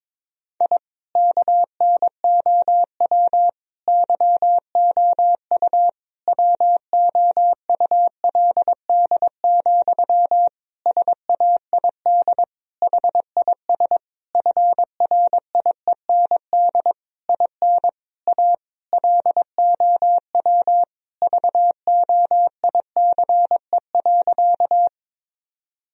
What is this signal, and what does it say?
Morse code, heard as a keyed tone, about 22 wpm, 700 Hz. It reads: I KNOW YOU WOULD, SAID HIS FRIEND IN A LOW VOICE.
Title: Sentences from Top 1000 Words - Review Part 1 - 22wpm